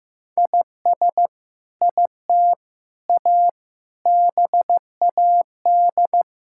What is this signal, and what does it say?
Morse code, heard as a keyed tone, about 15 wpm, 700 Hz. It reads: IS IT A BAD
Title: Sentences from Top 1000 Words - Review Part 5 - 15wpm